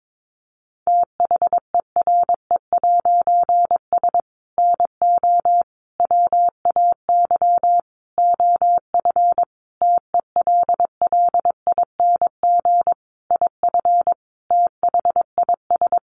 Morse code, heard as a keyed tone, about 22 wpm, 700 Hz. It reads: THERE'S NO WAY OF TELLING IF THIS
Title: A Sound Of Thunder 22wpm